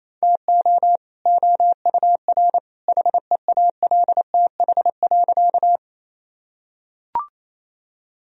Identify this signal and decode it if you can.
Morse code, heard as a keyed tone, about 28 wpm, 700 Hz. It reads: TO OUR HEALTH.
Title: Sentences from Top 1000 Words - Review Part 3 - 28wpm